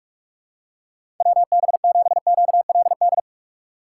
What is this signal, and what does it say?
Morse code, heard as a keyed tone, about 46 wpm, 700 Hz. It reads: WB7QLD
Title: USA Call Signs - 6 Characters - 2x Character Spacing - 45wpm